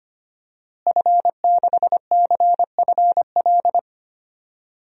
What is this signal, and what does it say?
Morse code, heard as a keyed tone, about 25 wpm, 700 Hz. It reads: F6CFL
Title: DXCC-100 Call Signs - 5 Characters - Repeated 2x - 25wpm